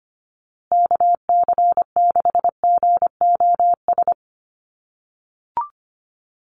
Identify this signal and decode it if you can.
Morse code, heard as a keyed tone, about 25 wpm, 700 Hz. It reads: KC6GOS
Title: USA Call Signs Repeated 3x 25wpm